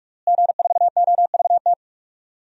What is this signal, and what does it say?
Morse code, heard as a keyed tone, about 46 wpm, 700 Hz. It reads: G4OVT
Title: DXCC-100 Call Signs - 5 Characters - Repeated 3x - 45wpm